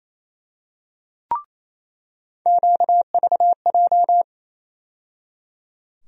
Morse code, heard as a keyed tone, about 28 wpm, 700 Hz. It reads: QVJ
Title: USA Call Sign Postfixes - 3 Letters 28wpm